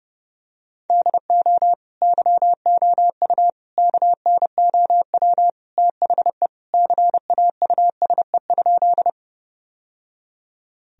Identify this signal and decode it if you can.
Morse code, heard as a keyed tone, about 30 wpm, 700 Hz. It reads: DO YOU KNOW THE CAUSE?
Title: Sentences from Top 1000 Words - Review Part 1 - 30wpm